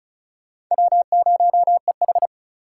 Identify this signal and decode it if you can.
Morse code, heard as a keyed tone, about 48 wpm, spaced as for 42 wpm, 700 Hz. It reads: W0EH
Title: Speed Racing - USA 1x2 Call Signs 35wpm